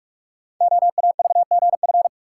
Morse code, heard as a keyed tone, about 44 wpm, 690 Hz. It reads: OAVGF
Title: Code Groups - 5 Letters - 45wpm